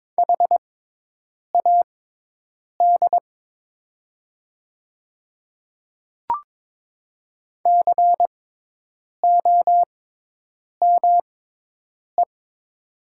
Morse code, heard as a keyed tone, about 22 wpm, spaced as for 8 wpm, 700 Hz. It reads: HAD COME
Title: 6x Character Spacing - Top 100 Words - 22wpm